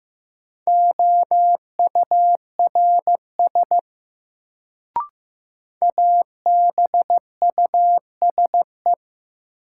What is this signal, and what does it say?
Morse code, heard as a keyed tone, about 15 wpm, 700 Hz. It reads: OURS ABUSE
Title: Sentences from Top 1300 Words - New Word Drill Alt - 15wpm